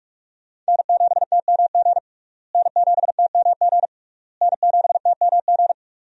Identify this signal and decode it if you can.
Morse code, heard as a keyed tone, about 46 wpm, 700 Hz. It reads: N7TMG N7TMG N7TMG
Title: USA Call Signs - 5 Characters - Repeated 3x - 45wpm